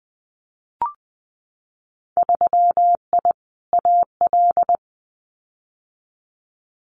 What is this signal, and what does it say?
Morse code, heard as a keyed tone, about 20 wpm, 700 Hz. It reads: 3I AL
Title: Winter Field Day Exchanges Encore 20wpm